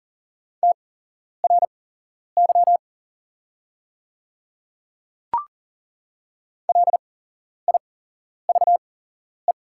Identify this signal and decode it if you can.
Morse code, heard as a keyed tone, about 40 wpm, spaced as for 11 wpm, 690 Hz. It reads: TRY LIVE
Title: 8x Character Spacing - Top 200 Words - 40wpm